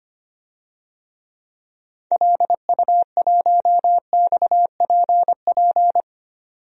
Morse code, heard as a keyed tone, about 25 wpm, 690 Hz. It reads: LU1XPP
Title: DXCC-100 Call Signs - 6 Characters - 25wpm